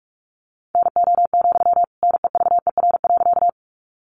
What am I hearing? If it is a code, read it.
Morse code, heard as a keyed tone, about 44 wpm, 700 Hz. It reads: NO, NEVER.
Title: Sentences from Top 300 Words 45wpm